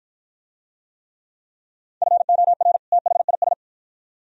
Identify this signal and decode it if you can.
Morse code, heard as a keyed tone, about 53 wpm, 700 Hz. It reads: FOR THIS